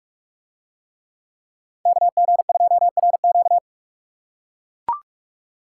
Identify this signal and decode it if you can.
Morse code, heard as a keyed tone, about 40 wpm, 700 Hz. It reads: KG2RQ